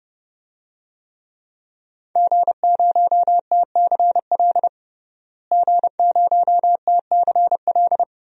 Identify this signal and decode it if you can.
Morse code, heard as a keyed tone, about 30 wpm, 700 Hz. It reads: G0TCL G0TCL
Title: DXCC-100 Call Signs - 5 Characters - Repeated 2x - 30wpm